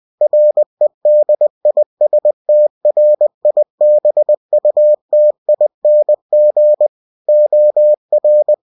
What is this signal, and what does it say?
Morse code, heard as a keyed tone, about 20 wpm, 600 Hz. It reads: REDISTRIBUTING OR
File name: winnie-20wpm